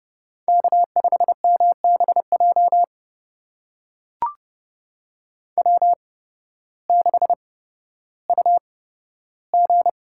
Morse code, heard as a keyed tone, about 30 wpm, spaced as for 14 wpm, 700 Hz. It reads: K5MBJ W6UG
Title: USA Call Signs - 8x Character Spacing - 30wpm